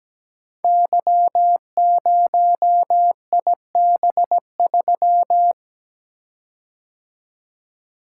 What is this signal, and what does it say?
Morse code, heard as a keyed tone, about 17 wpm, 700 Hz. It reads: Y0IB3
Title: Code Groups - 5 Letter:Numbers - 17wpm